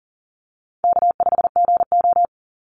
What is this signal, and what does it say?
Morse code, heard as a keyed tone, about 38 wpm, 700 Hz. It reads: K5GO
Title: Easy Call Signs - 40wpm